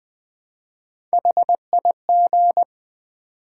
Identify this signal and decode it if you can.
Morse code, heard as a keyed tone, about 20 wpm, 700 Hz. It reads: HIG
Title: Build A Word - 1 Letter At A Time - 4-Letter Words - 20wpm